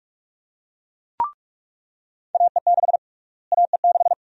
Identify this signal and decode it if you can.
Morse code, heard as a keyed tone, about 46 wpm, 700 Hz. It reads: AE6 AE6
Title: USA Call Sign Prefixes - Repeated 2x - 45wpm